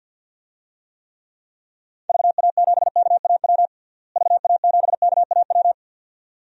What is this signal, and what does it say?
Morse code, heard as a keyed tone, about 53 wpm, 700 Hz. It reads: VA7KAW VA7KAW
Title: DXCC-100 Call Signs - 6 Characters - Repeated 2x - 50wpm